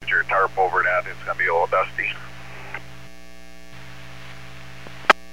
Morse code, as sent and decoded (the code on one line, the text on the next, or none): none